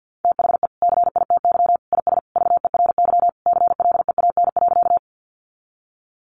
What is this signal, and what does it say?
Morse code, heard as a keyed tone, about 50 wpm, 700 Hz. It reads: THE CITY IS VERY CLEAN.
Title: Easy Phrases 50wpm